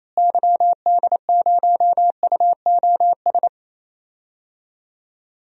Morse code, heard as a keyed tone, about 28 wpm, 700 Hz. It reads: YD0UOS
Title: DXCC-100 Call Signs - 6 Characters - Repeated 2x - 28wpm